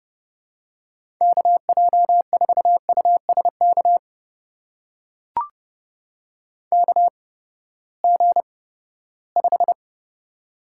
Morse code, heard as a keyed tone, about 30 wpm, spaced as for 17 wpm, 700 Hz. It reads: KJ4USK KG5
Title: USA Call Signs - 6 Characters - 8x Character Spacing - 30wpm